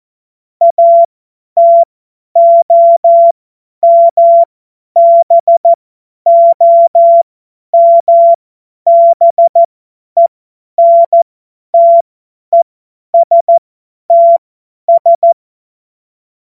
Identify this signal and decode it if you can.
Morse code, heard as a keyed tone, about 14 wpm, spaced as for 10 wpm, 680 Hz. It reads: ATOMBOMBENTESTS
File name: Morsekurs / Witze / Witz 11. Januar 2022, 50 F70